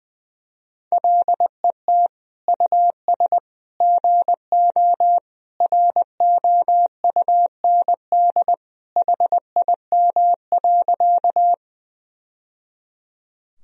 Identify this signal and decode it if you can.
Morse code, heard as a keyed tone, about 20 wpm, 700 Hz. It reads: LET US GO ROUND HIM.
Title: Sentences from Top 1000 Words - Review Part 1 - 20wpm